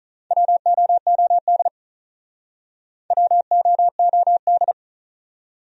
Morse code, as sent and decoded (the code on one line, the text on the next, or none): .-- --- --- -.. / .-- --- --- -..
WOOD WOOD